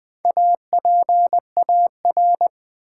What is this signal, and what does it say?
Morse code, heard as a keyed tone, about 20 wpm, 700 Hz. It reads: APAR
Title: Build A Word - 1 Letter At A Time - 5-Letter Words - 20wpm